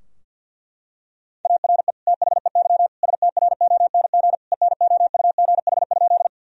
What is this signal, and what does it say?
Morse code, heard as a keyed tone, about 50 wpm, 700 Hz. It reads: ARE THEY STRONG ENOUGH?
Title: Sentences from Top 1000 Words - Review Part 3 - 50wpm